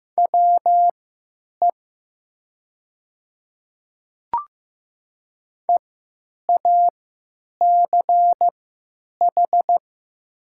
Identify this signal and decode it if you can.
Morse code, heard as a keyed tone, about 15 wpm, spaced as for 9 wpm, 700 Hz. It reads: WE EACH
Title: 3x Character Spacing - Top 200 Words - 15wpm